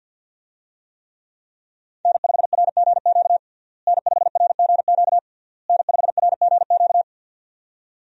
Morse code, heard as a keyed tone, about 50 wpm, 700 Hz. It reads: N5RGQ N5RGQ N5RGQ
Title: USA Call Signs - 5 Characters - Repeated 3x - 50wpm